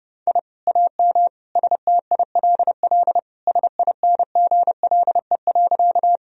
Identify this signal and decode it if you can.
Morse code, heard as a keyed tone, about 32 wpm, 700 Hz. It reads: I AM STILL SINGLE.
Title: Sentences from Top 1000 Words - Review Part 4 - 30wpm